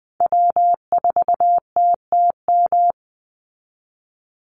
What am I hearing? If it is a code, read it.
Morse code, heard as a keyed tone, about 20 wpm, 700 Hz. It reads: W4TTM
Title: Call Signs Repeated 3x - 20wpm